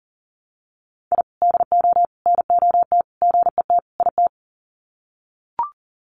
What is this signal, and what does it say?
Morse code, heard as a keyed tone, about 40 wpm, 700 Hz. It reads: I DO NOT GET IT
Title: Sets of 5 Words Encore 40wpm